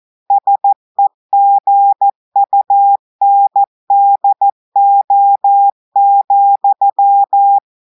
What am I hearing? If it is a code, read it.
Morse code, heard as a keyed tone, about 14 wpm, 800 Hz. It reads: SEGUNDO,
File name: audios / portugais / portugais texte en clair 1 vitesse 14 mots minutes